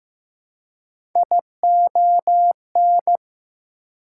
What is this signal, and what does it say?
Morse code, heard as a keyed tone, about 15 wpm, 700 Hz. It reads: ION
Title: Prefixes and Suffixes 15wpm